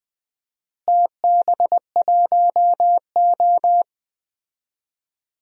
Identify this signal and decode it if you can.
Morse code, heard as a keyed tone, about 20 wpm, 700 Hz. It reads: TB1O
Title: DXCC-100 Call Signs - 4 Characters - Repeated 3x - 20wpm